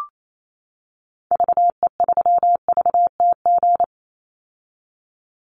Morse code, heard as a keyed tone, about 28 wpm, 700 Hz. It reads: VE3VTG
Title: Call Signs - 28wpm